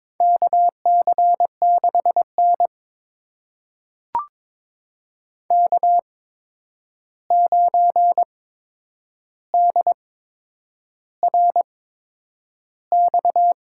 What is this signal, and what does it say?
Morse code, heard as a keyed tone, about 22 wpm, spaced as for 9 wpm, 700 Hz. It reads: KC6N K9DRX
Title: USA Call Signs - 8x Character Spacing - 22wpm